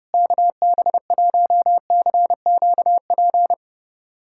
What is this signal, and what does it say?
Morse code, heard as a keyed tone, about 30 wpm, 700 Hz. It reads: KB1CQP
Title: USA Callsign DE Callsign 30wpm